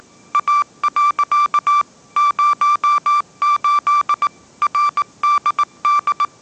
Morse code, heard as a keyed tone, about 19 wpm, 1170 Hz. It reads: A. 08 RDD
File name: 20wpm